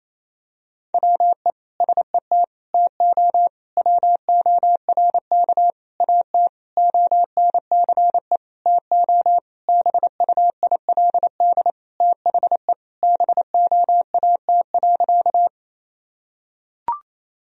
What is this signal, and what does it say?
Morse code, heard as a keyed tone, about 28 wpm, 700 Hz. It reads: WE SET TO WORK AT ONCE TO BUILD THE BOAT.
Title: Sentences from Top 1000 Words - Review Part 1 - 28wpm